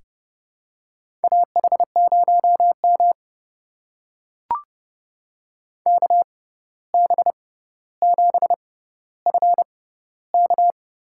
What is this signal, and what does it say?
Morse code, heard as a keyed tone, about 30 wpm, spaced as for 14 wpm, 700 Hz. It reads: AH0M KB7FK
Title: USA Call Signs - 6x Character Spacing - 30wpm